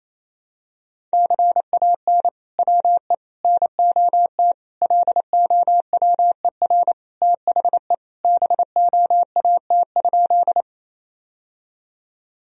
Morse code, heard as a keyed tone, about 28 wpm, 700 Hz. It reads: CAN WE NOT LOWER THE BOAT?
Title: Sentences from Top 1000 Words - Review Part 4 - 28wpm